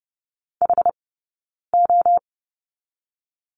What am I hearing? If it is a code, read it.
Morse code, heard as a keyed tone, about 30 wpm, spaced as for 9 wpm, 690 Hz. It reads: HO
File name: course / 30:10wpm / 086 Lesson 15d - Number 5 - Two Characters 30wpm